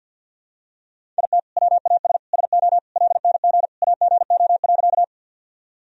Morse code, heard as a keyed tone, about 53 wpm, 700 Hz. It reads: IT WAS SO LNG AGO.
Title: Sentences from Top 500 Words and Abbreviations 50wpm